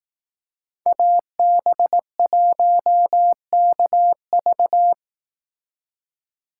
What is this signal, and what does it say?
Morse code, heard as a keyed tone, about 19 wpm, 700 Hz. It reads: AB1KV